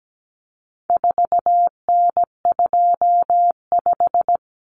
Morse code, heard as a keyed tone, about 17 wpm, 700 Hz. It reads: 4N25
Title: Common Cut Numbers - 4-Digit 17wpm